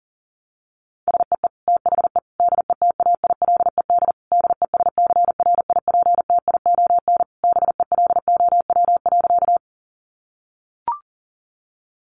Morse code, heard as a keyed tone, about 40 wpm, 700 Hz. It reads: SEE THE DETAILED DESCRIPTION BELOW.
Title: Sentences from Top 1500 Words 40wpm